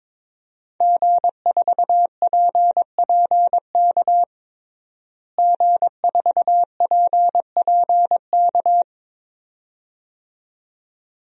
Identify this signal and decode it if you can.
Morse code, heard as a keyed tone, about 22 wpm, 700 Hz. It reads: G4PPK G4PPK
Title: DXCC-100 Call Signs - 5 Characters - Repeated 2x - 22wpm